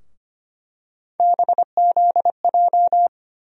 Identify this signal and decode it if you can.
Morse code, heard as a keyed tone, about 25 wpm, 700 Hz. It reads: BZJ
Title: USA Call Sign Postfixes - 3 Letters 25wpm